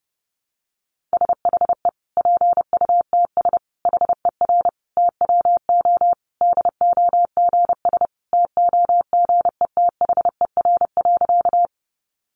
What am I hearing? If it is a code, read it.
Morse code, heard as a keyed tone, about 30 wpm, 700 Hz. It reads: SHE PUTS HER TWO DOGS TOGETHER.
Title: Easy Phrases 30wpm